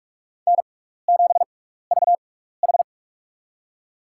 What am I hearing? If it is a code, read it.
Morse code, heard as a keyed tone, about 43 wpm, spaced as for 15 wpm, 700 Hz. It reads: N7VH
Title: USA Call Signs - 4 Characters - 6x Character Spacing - 45wpm